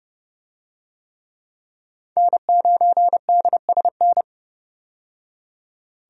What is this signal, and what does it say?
Morse code, heard as a keyed tone, about 30 wpm, 700 Hz. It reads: N9DSN
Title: USA Call Signs - 5 Characters - 30wpm